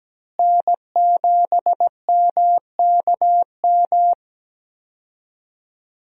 Morse code, heard as a keyed tone, about 17 wpm, 700 Hz. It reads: N7MKM
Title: USA Call Signs - 5 Characters - Repeated 2x - 17wpm